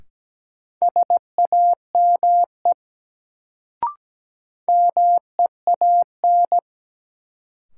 Morse code, heard as a keyed tone, about 17 wpm, 700 Hz. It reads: SAME MEAN
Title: Sentences from Top 200 Words - New Word Drill - 17wpm